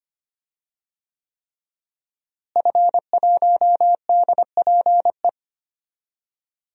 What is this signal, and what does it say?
Morse code, heard as a keyed tone, about 25 wpm, 700 Hz. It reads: F1DPE
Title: DXCC-100 Call Signs - 5 Characters - Repeated 3x - 25wpm